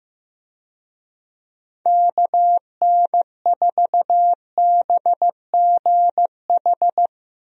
Morse code, heard as a keyed tone, about 16 wpm, 700 Hz. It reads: KN4BGH
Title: USA Call Signs - 6 Characters - 15wpm